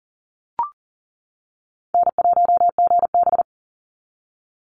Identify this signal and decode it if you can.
Morse code, heard as a keyed tone, about 40 wpm, 700 Hz. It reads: N1GB
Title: NA SOTA Chaser Callsigns 40wpm